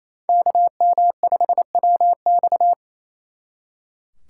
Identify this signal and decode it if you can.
Morse code, heard as a keyed tone, about 37 wpm, 700 Hz. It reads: KM5WX